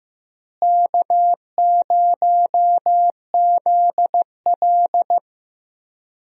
Morse code, heard as a keyed tone, about 15 wpm, 700 Hz. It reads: K0ZL
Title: USA Call Signs - 4 Characters - Repeated 2x - 15wpm